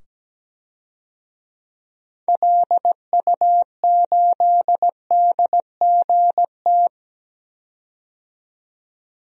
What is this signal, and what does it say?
Morse code, heard as a keyed tone, about 17 wpm, 700 Hz. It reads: LU8DGT
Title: DXCC-100 Call Signs - 6 Characters - 17wpm